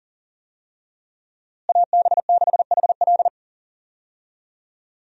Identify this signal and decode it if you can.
Morse code, heard as a keyed tone, about 37 wpm, 700 Hz. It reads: AB6HL
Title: USA Call Signs - 5 Characters - Repeated 3x - 40wpm